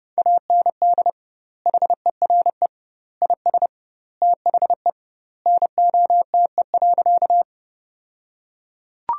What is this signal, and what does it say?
Morse code, heard as a keyed tone, about 31 wpm, 700 Hz. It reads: AND HERE IS THE NOTE.
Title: 2x Word-Spacing - Sentences from Top 500 Words - 30wpm